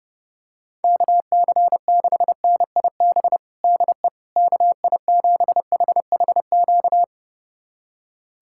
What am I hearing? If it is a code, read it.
Morse code, heard as a keyed tone, about 29 wpm, 700 Hz. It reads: KC6NIB DE KI7HHQ
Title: USA Callsign DE Callsign - 6 Characters - 30wpm